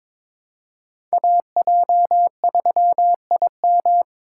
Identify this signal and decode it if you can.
Morse code, heard as a keyed tone, about 22 wpm, 700 Hz. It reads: AJ3IM
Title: USA Call Signs - 5 Characters - 3x Character Spacing - 22wpm